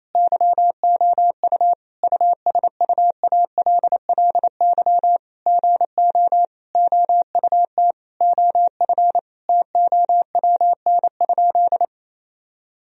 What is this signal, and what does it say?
Morse code, heard as a keyed tone, about 28 wpm, 700 Hz. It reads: YOU USUALLY GO OUT OF TOWN?
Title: Sentences from Top 1000 Words - Review Part 3 - 28wpm